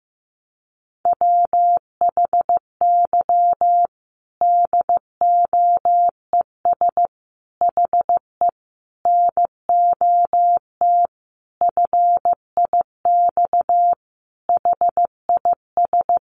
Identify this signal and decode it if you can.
Morse code, heard as a keyed tone, about 15 wpm, 700 Hz. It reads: WHY DOES HE NOT FIX HIS
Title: Sentences from Top 1400 Words 15wpm